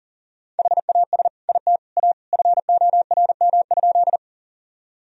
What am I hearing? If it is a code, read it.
Morse code, heard as a keyed tone, about 40 wpm, 700 Hz. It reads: HAS IT A FORM?
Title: Sentences from Top 1000 Words - Review Part 1 - 40wpm